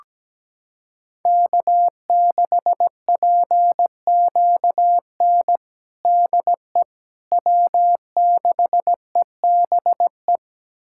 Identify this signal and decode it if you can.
Morse code, heard as a keyed tone, about 17 wpm, 700 Hz. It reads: K6PQN DE W6EBE
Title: USA Callsign DE Callsign - 5 Characters - 17wpm